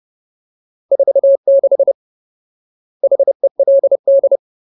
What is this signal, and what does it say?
Morse code, heard as a keyed tone, about 30 wpm, 550 Hz. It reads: 46 HELD
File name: Ultimate Word List / 30WPM Ultimate